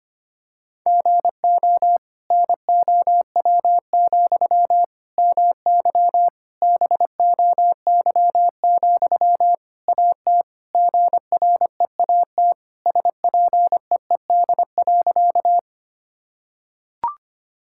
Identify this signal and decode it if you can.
Morse code, heard as a keyed tone, about 25 wpm, 700 Hz. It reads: GO NOW, MY BOY, AT GREAT SPEED.
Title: Sentences from Top 1000 Words - Review Part 5 - 25wpm